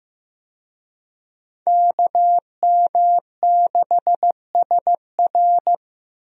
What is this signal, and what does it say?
Morse code, heard as a keyed tone, about 15 wpm, 700 Hz. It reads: KM6SR
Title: USA Call Signs - 5 Characters - 15wpm